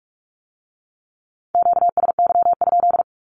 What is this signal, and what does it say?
Morse code, heard as a keyed tone, about 46 wpm, 700 Hz. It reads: QSY?
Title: Q-Codes with Spoken Meaning - 45wpm